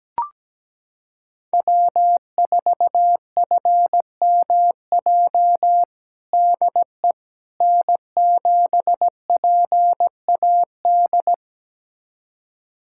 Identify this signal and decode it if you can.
Morse code, heard as a keyed tone, about 17 wpm, 700 Hz. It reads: W4FMJ DE N7PAD
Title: USA Callsign DE Callsign - 5 Characters - 17wpm